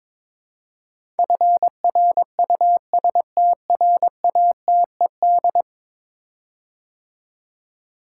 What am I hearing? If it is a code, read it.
Morse code, heard as a keyed tone, about 23 wpm, 700 Hz. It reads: FRUSTRATED
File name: practice / 22wpm / 10 Letter Words - First 3 Letters Given - 22wpm